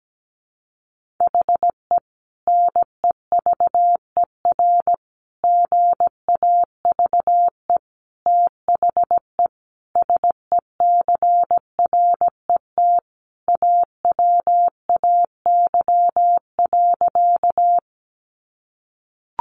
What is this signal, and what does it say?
Morse code, heard as a keyed tone, about 17 wpm, 700 Hz. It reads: HE NEVER GAVE THE SECRET AWAY.
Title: Sentences from Top 1200 Words 17wpm